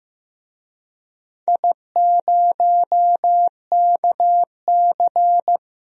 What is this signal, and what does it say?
Morse code, heard as a keyed tone, about 15 wpm, 700 Hz. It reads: I0KC
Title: DXCC-100 Call Signs - 4 Characters - Repeated 2x - 15wpm